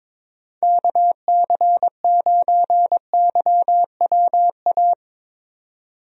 Morse code, heard as a keyed tone, about 22 wpm, 700 Hz. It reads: KC9YWA
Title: USA Call Signs - 6 Characters - Repeated 2x - 22wpm